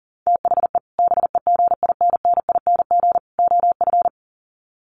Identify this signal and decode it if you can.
Morse code, heard as a keyed tone, about 40 wpm, 700 Hz. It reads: THE BEGINNING OF
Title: Sets of 3 Words 40wpm